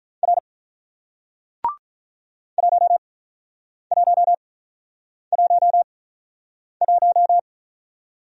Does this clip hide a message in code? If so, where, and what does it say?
Yes: Morse code from 0.2 to 7.4 s, heard as a keyed tone, about 47 wpm, spaced as for 9 wpm, 700 Hz. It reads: R 1111